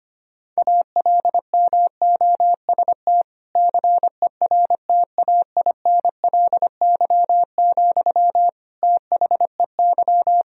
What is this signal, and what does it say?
Morse code, heard as a keyed tone, about 25 wpm, 700 Hz. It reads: ALMOST CERTAINLY, THEY
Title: Sentences from Top 1000 Words - Review Part 4 - 25wpm